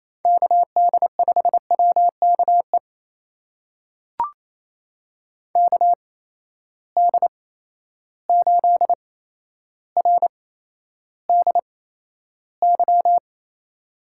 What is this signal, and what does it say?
Morse code, heard as a keyed tone, about 28 wpm, spaced as for 12 wpm, 700 Hz. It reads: KD5WKE KD8RDY
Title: USA Call Signs - 6 Characters - 8x Character Spacing - 28wpm